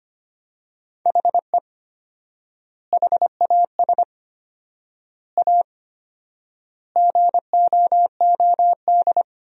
Morse code, heard as a keyed tone, about 25 wpm, 700 Hz. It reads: HE HAS A GOOD
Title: 4x Word-Spacing - Sentences from Top 500 Words - 25wpm